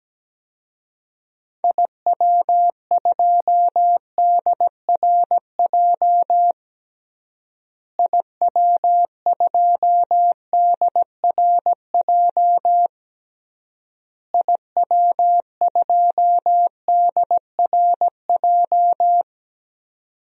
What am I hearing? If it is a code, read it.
Morse code, heard as a keyed tone, about 17 wpm, 700 Hz. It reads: IW2DRJ IW2DRJ IW2DRJ
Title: DXCC-100 Call Signs - 6 Characters - Repeated 3x - 17wpm